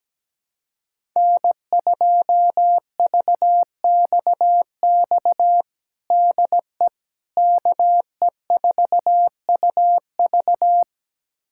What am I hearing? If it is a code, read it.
Morse code, heard as a keyed tone, about 17 wpm, 700 Hz. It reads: N2VXX DE KE4UV
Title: USA Callsign DE Callsign - 5 Characters - 17wpm